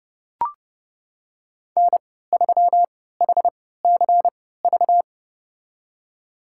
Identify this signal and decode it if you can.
Morse code, heard as a keyed tone, about 31 wpm, spaced as for 17 wpm, 700 Hz. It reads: N3HCV
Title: USA Call Signs - 5 Characters - 3x Character Spacing - 30wpm